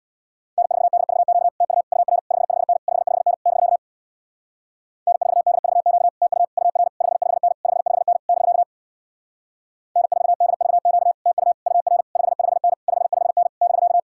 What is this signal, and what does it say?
Morse code, heard as a keyed tone, about 67 wpm, 700 Hz. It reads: N4DVQ TU UR 55N 55N <BK> N4DVQ TU UR 55N 55N <BK> N4DVQ TU UR 55N 55N <BK>